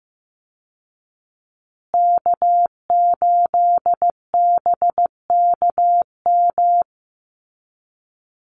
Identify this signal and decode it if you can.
Morse code, heard as a keyed tone, about 15 wpm, 700 Hz. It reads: K8BKM
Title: Call Signs Repeated 3x - 15wpm